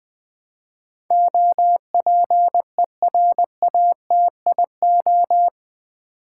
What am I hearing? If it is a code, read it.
Morse code, heard as a keyed tone, about 20 wpm, 700 Hz. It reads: OPERATIO
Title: Build A Word - 4 Letters At A Time - 11-Letter Words - 20wpm